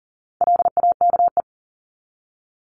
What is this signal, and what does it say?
Morse code, heard as a keyed tone, about 39 wpm, 700 Hz. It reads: LAKE
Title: Top 1001-1250 Words 40wpm